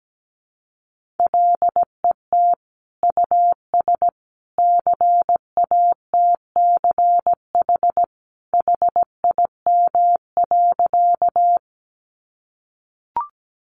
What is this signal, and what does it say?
Morse code, heard as a keyed tone, about 17 wpm, 700 Hz. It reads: LET US CATCH HIM.
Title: Sentences from Top 1500 Words 17wpm